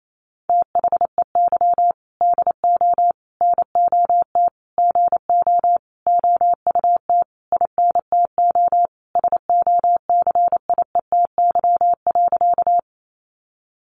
Morse code, heard as a keyed tone, about 28 wpm, 700 Hz. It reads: THEY DO NOT GO OUT INTO SOCIETY.
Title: Sentences from Top 700 Words 28wpm